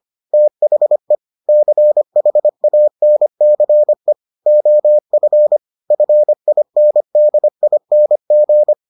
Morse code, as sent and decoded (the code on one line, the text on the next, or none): - .... / -.-. .... .- -. -.-. / --- ..-. / ..-. .. -. -.. .. -. --.
THE CHANCE OF FINDING